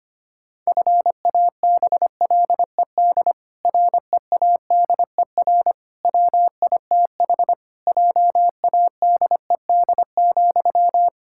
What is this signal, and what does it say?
Morse code, heard as a keyed tone, about 25 wpm, 700 Hz. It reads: FABLED READER WITH JADED,
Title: Pangram Challenge v2 - 25wpm